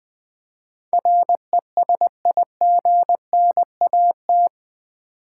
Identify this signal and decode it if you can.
Morse code, heard as a keyed tone, about 20 wpm, 700 Hz. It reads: RESIGNAT
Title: Build A Word - 4 Letters At A Time - 11-Letter Words - 20wpm